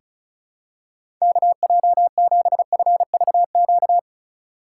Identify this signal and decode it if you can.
Morse code, heard as a keyed tone, about 34 wpm, 700 Hz. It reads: KJ7FVQ
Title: USA Call Signs - 4x Character Spacing - 35wpm